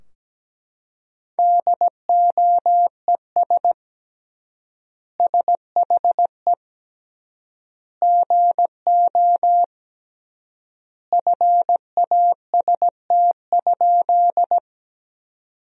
Morse code, heard as a keyed tone, about 17 wpm, 700 Hz. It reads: DOES SHE GO FAST?
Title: 3x Word-Spacing - Sentences from Top 400 Words - 17wpm